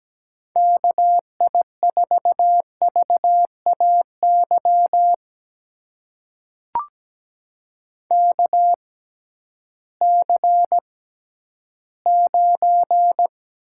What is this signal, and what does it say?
Morse code, heard as a keyed tone, about 17 wpm, spaced as for 11 wpm, 700 Hz. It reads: KI4VAY KC9
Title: USA Call Signs - 6x Character Spacing - 17wpm